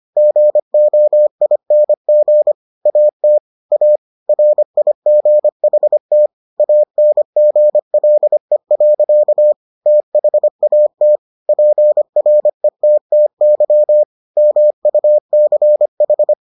Morse code, heard as a keyed tone, about 25 wpm, 600 Hz. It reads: GOING AT A RIGHT ANGLE. THAT PRETTY MUCH